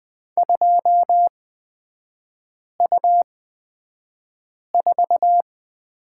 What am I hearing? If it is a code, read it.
Morse code, heard as a keyed tone, about 20 wpm, spaced as for 5 wpm, 700 Hz. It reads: 2U4
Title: 236 Lesson 38e - Letter q - Three Characters 20wpm